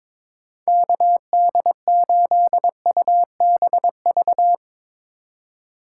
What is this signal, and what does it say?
Morse code, heard as a keyed tone, about 22 wpm, 700 Hz. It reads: KD8UBV
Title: USA Call Signs - 6 Characters - 2x Character Spacing - 22wpm